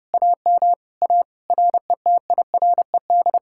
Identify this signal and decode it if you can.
Morse code, heard as a keyed tone, about 38 wpm, 700 Hz. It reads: AM A RETIRED